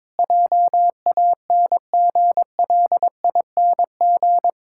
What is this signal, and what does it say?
Morse code, heard as a keyed tone, about 22 wpm, 700 Hz. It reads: JANGLING
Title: Pangram Challenge v2 - 22wpm